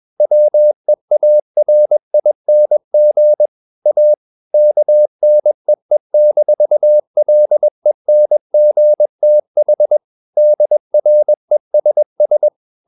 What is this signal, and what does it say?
Morse code, heard as a keyed tone, about 21 wpm, 600 Hz. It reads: WEARING A KNEE-LENGTH DRESS
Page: CW med Gnister